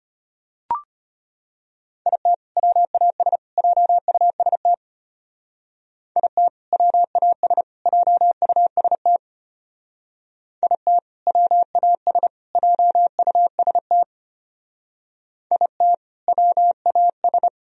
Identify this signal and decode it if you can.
Morse code, heard as a keyed tone, about 32 wpm, 700 Hz. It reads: IT WAS JUST IT WAS JUST IT WAS JUST IT WAS